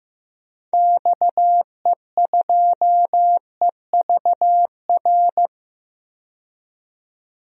Morse code, heard as a keyed tone, about 15 wpm, 700 Hz. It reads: XE2EVR
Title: DXCC-100 Call Signs - 6 Characters - 15wpm